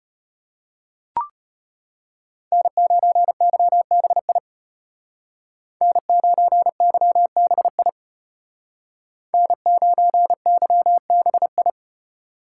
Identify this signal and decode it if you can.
Morse code, heard as a keyed tone, about 30 wpm, 700 Hz. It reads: N9YBI N9YBI N9YBI
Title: Speed Racing - USA 1x3 Call Signs 25wpm